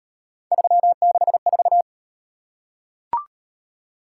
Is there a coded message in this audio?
Yes: Morse code from 0.5 to 1.8 s, heard as a keyed tone, about 38 wpm, 700 Hz. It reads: 364